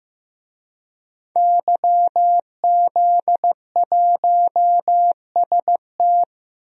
Morse code, heard as a keyed tone, about 15 wpm, 700 Hz. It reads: YZ1ST
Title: DXCC-100 Call Signs - 5 Characters - Repeated 3x - 15wpm